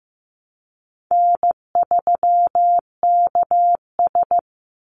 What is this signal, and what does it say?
Morse code, heard as a keyed tone, about 15 wpm, 700 Hz. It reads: N3KS
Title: Call Signs Repeated 3x - 15wpm